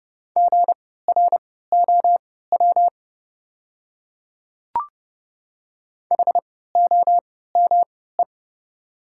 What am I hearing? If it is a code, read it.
Morse code, heard as a keyed tone, about 30 wpm, spaced as for 17 wpm, 690 Hz. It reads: GROW HOME
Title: 3x Character Spacing - Top 500 Words - 30wpm